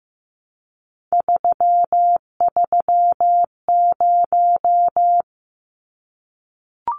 Morse code, heard as a keyed tone, about 15 wpm, 700 Hz. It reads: 330
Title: Numbers - 3-Digit 15wpm